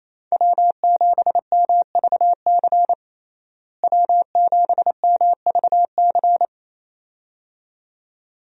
Morse code, heard as a keyed tone, about 27 wpm, 700 Hz. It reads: W7MVC W7MVC
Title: USA Call Signs - 5 Characters - Repeated 2x - 28wpm